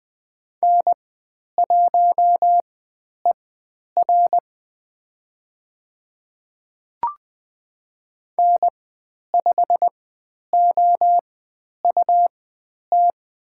Morse code, heard as a keyed tone, about 20 wpm, spaced as for 10 wpm, 700 Hz. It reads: N1ER N5OUT
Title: 109 Lesson 18g - Letter m - Call Signs 20wpm